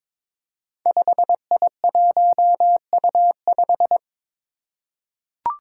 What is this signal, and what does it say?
Morse code, heard as a keyed tone, about 22 wpm, 700 Hz. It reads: 5I1U5
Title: Five Challenging Letter-Number Combinations 22wpm